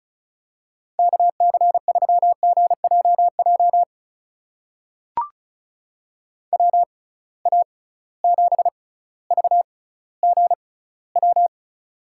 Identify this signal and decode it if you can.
Morse code, heard as a keyed tone, about 36 wpm, spaced as for 18 wpm, 700 Hz. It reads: KC3GJJ WA7VGW
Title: USA Call Signs - 6 Characters - 6x Character Spacing - 35wpm